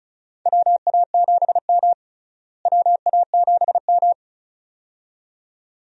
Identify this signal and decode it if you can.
Morse code, heard as a keyed tone, about 36 wpm, 700 Hz. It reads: WA7M WA7M
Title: USA Call Signs - 4 Characters - Repeated 2x - 35wpm